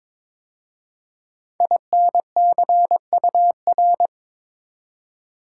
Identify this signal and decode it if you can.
Morse code, heard as a keyed tone, about 22 wpm, 700 Hz. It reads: INCUR
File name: 5 Letter Words - First 3 Letters Given - 22wpm